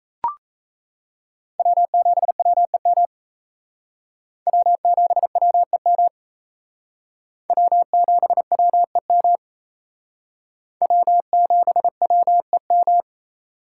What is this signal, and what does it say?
Morse code, heard as a keyed tone, about 35 wpm, 700 Hz. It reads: W7WEM W7WEM W7WEM W7WEM
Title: Speed Racing - USA 1x3 Call Signs 28wpm